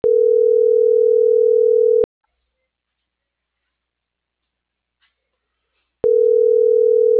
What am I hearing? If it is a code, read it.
Ringback tone (North America), 440 + 480 Hz.